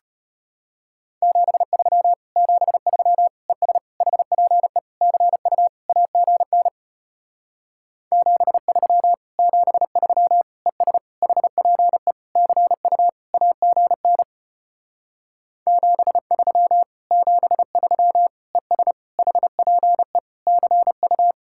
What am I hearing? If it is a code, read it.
Morse code, heard as a keyed tone, about 33 wpm, 700 Hz. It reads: 73 73 ES HPE CU AGN 73 73 ES HPE CU AGN 73 73 ES HPE CU